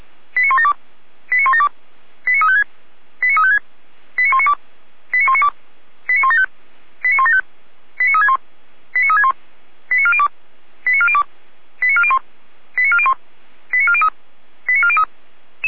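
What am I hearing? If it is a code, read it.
Five-tone selective call (ZVEI), heard as quick runs of five tones, tones of about 80 ms. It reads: 89172 89172 89356 89356 89192 89192 89175 89175 89361 89361 89402 89402 89401 89401 89403 89403
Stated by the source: ZVEI five-tone selective call 89582 received by SDR, recorded by Feuerwehr Orlamünde.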